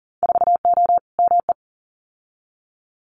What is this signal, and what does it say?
Morse code, heard as a keyed tone, about 41 wpm, 700 Hz. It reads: 4O ME
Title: Winter Field Day Exchanges Encore 40wpm